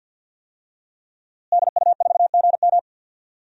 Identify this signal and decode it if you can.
Morse code, heard as a keyed tone, about 52 wpm, 700 Hz. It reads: DU4GM
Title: DXCC-100 Call Signs - 5 Characters - Repeated 3x - 50wpm